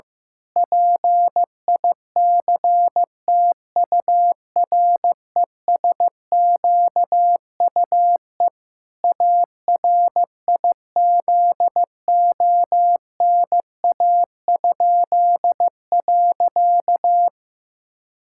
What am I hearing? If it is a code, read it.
Morse code, heard as a keyed tone, about 15 wpm, 700 Hz. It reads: PICTURESQUE ARIZONA?.
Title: Pangram Challenge v2 - 15wpm